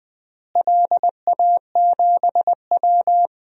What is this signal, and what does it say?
Morse code, heard as a keyed tone, about 20 wpm, 700 Hz. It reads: LA7W
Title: DXCC-100 Call Signs - 4 Characters - Repeated 3x - 20wpm